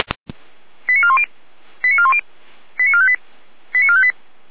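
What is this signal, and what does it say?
Five-tone selective call (ZVEI), heard as quick runs of five tones, tones of about 70 ms. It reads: 89410 89410 89458 89458
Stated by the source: Five-tone ZVEI selective call 88801 received by SDR.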